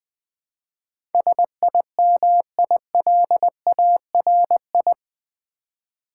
Morse code, heard as a keyed tone, about 20 wpm, 700 Hz. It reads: SIMILARI
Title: Build A Word - 4 Letters At A Time - 10-Letter Words - 20wpm